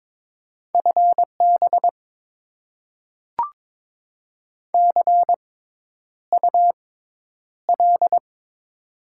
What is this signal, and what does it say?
Morse code, heard as a keyed tone, about 22 wpm, spaced as for 10 wpm, 700 Hz. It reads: FB CUL
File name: Top 90 QSO Elements - 6x Character Spacing - 22wpm